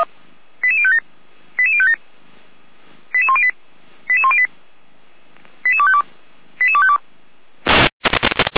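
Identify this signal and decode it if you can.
Five-tone selective call (ZVEI), heard as quick runs of five tones, tones of about 70 ms. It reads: 88057 88057 88108 88108 88252 88252
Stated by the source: Five-tone ZVEI selective call received on an SDR, fire brigade alerting.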